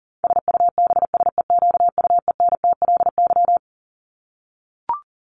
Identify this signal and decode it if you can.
Morse code, heard as a keyed tone, about 37 wpm, 700 Hz. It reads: SUBSEQUENTLY
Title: LY Suffix - Top 100 Words with Base Words 40wpm